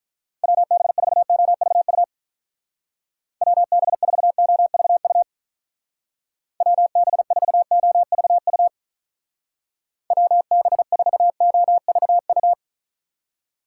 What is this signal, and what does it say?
Morse code, heard as a keyed tone, about 44 wpm, 700 Hz. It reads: WB4OVU WB4OVU WB4OVU WB4OVU